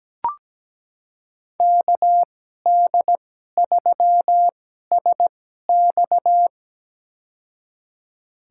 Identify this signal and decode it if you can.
Morse code, heard as a keyed tone, about 17 wpm, spaced as for 12 wpm, 700 Hz. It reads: KD3SX
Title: USA Call Signs - 5 Characters - 2x Character Spacing - 17wpm